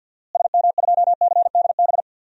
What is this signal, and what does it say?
Morse code, heard as a keyed tone, about 53 wpm, 700 Hz. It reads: SM2CDB
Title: DXCC-100 Call Signs - 6 Characters - Repeated 2x - 50wpm